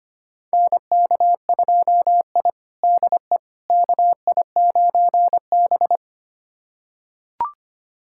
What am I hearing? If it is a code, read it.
Morse code, heard as a keyed tone, about 25 wpm, 700 Hz. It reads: NK2I DE KI9B
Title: USA Callsign DE Callsign - 4 Characters - 25wpm